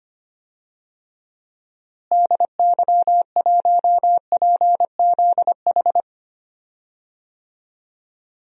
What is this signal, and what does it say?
Morse code, heard as a keyed tone, about 25 wpm, 700 Hz. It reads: DY1PZH
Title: DXCC-100 Call Signs - 6 Characters - Repeated 3x - 25wpm